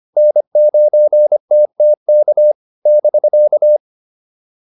Morse code, heard as a keyed tone, about 25 wpm, 600 Hz. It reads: N9TTK <BK>
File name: QSO205 25